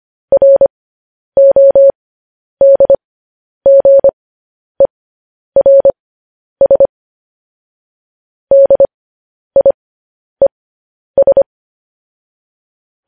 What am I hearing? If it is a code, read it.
Morse code, heard as a keyed tone, about 25 wpm, spaced as for 10 wpm, 550 Hz. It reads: RODGERS DIES
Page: news headlines in morse code at 10 wpm